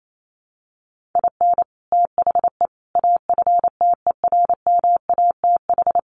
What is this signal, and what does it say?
Morse code, heard as a keyed tone, about 28 wpm, 700 Hz. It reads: IN THE AFTERMATH
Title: Sets of 5 Words 28wpm